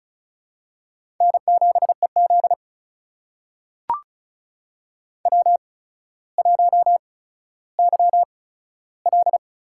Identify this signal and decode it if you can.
Morse code, heard as a keyed tone, about 36 wpm, spaced as for 15 wpm, 700 Hz. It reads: N7EZ W1YL
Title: USA Call Signs - 4 Characters - 8x Character Spacing - 35wpm